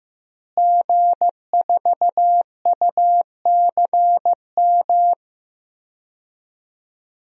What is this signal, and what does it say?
Morse code, heard as a keyed tone, about 16 wpm, 700 Hz. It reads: G4UCM
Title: DXCC-100 Call Signs - 5 Characters - Repeated 2x - 15wpm